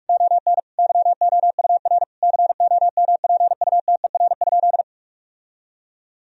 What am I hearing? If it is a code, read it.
Morse code, heard as a keyed tone, about 41 wpm, 700 Hz. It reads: ON YOUR COMPUTER?
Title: Sentences from Top 1000 Words - Review Part 3 - 45wpm